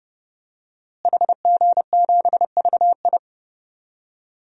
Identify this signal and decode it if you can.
Morse code, heard as a keyed tone, about 30 wpm, 700 Hz. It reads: HG7VI
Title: DXCC-100 Call Signs - 5 Characters - Repeated 3x - 30wpm